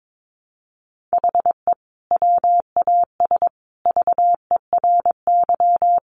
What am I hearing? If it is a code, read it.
Morse code, heard as a keyed tone, about 22 wpm, 700 Hz. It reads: HE WAS VERY
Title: Sentences from Top 1400 Words 22wpm